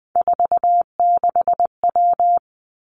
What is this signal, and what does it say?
Morse code, heard as a keyed tone, about 20 wpm, 700 Hz. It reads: 46W
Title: Three Challenging Letter-Number Combinations 20wpm